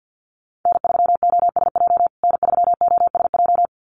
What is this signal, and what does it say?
Morse code, heard as a keyed tone, about 50 wpm, 700 Hz. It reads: N3OSJ N3OSJ
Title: USA Call Signs Repeated 2x 50wpm